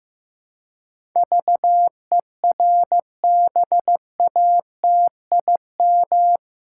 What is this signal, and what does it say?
Morse code, heard as a keyed tone, about 15 wpm, 700 Hz. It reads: VERBATIM
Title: Pangram Challenge v2 - 15wpm